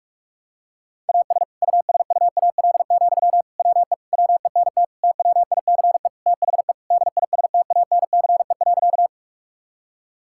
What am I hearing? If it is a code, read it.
Morse code, heard as a keyed tone, about 46 wpm, 700 Hz. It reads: AS USUAL, WE WENT TWICE THE DISTANCE.
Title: Sentences from Top 1600 Words 45wpm